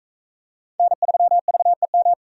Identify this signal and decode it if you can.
Morse code, heard as a keyed tone, about 38 wpm, 700 Hz. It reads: N3VEM